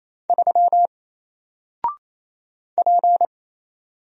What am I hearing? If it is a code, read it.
Morse code, heard as a keyed tone, about 27 wpm, 700 Hz. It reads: 3 P
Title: Single Letter-Number 28wpm